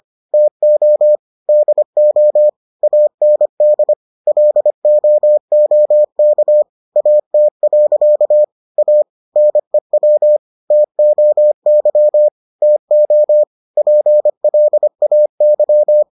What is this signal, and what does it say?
Morse code, heard as a keyed tone, about 25 wpm, 600 Hz. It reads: TO DO AND LOOK AT. A NEW TOY TO PLAY